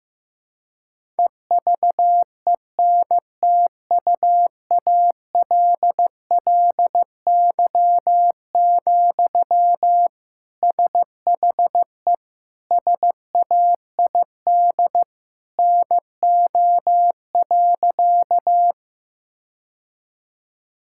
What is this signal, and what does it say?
Morse code, heard as a keyed tone, about 15 wpm, 700 Hz. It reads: EVENTUALLY, SHE SAID NO.
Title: Sentences from Top 1000 Words - Review Part 6 - 15wpm